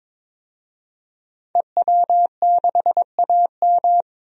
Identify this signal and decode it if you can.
Morse code, heard as a keyed tone, about 22 wpm, 700 Hz. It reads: EW6AM
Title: DXCC-100 Call Signs - 5 Characters - 22wpm